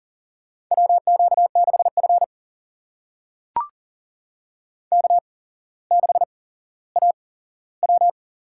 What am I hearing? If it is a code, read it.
Morse code, heard as a keyed tone, about 40 wpm, spaced as for 17 wpm, 690 Hz. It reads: WQ6F K6AW
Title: USA Call Signs - 4 Characters - 8x Character Spacing - 40wpm